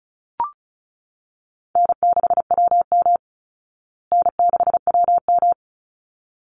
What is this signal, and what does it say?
Morse code, heard as a keyed tone, about 34 wpm, 700 Hz. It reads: N6WM N6WM
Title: Easy Call Signs Repeated 2x - 35wpm